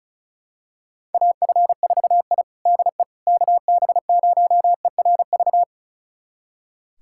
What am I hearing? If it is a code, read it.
Morse code, heard as a keyed tone, about 34 wpm, 700 Hz. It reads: AF4I DE KB0ERV
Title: USA Callsign DE Callsign 35wpm